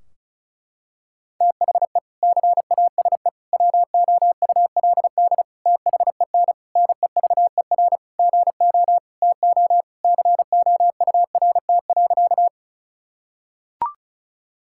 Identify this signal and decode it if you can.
Morse code, heard as a keyed tone, about 34 wpm, 700 Hz. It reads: THE CASE WOULD THEN NEVER GO TO COURT.
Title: Sentences from Top 1600 Words 35wpm